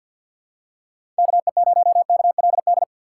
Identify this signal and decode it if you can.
Morse code, heard as a keyed tone, about 53 wpm, 700 Hz. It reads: KE0KLD
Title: USA Call Signs - 3x Character Spacing - 50wpm